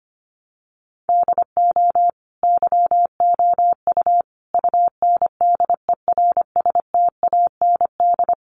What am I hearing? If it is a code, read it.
Morse code, heard as a keyed tone, about 25 wpm, 700 Hz. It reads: DO YOU UNDERSTAND
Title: Sentences from Top 600 Words 25wpm